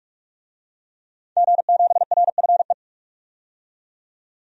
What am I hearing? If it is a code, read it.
Morse code, heard as a keyed tone, about 41 wpm, 700 Hz. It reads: G7RFE
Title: DXCC-100 Call Signs - 5 Characters - 45wpm